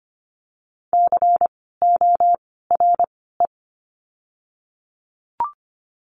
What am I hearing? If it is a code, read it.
Morse code, heard as a keyed tone, about 25 wpm, spaced as for 16 wpm, 700 Hz. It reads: CORE
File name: course / 25:10wpm / 094 Lesson 16f - Letter c - Words 25wpm